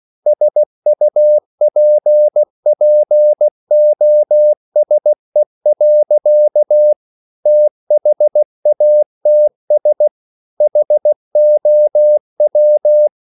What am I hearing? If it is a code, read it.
Morse code, heard as a keyed tone, about 16 wpm, 600 Hz. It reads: SUPPOSE. THATS HOW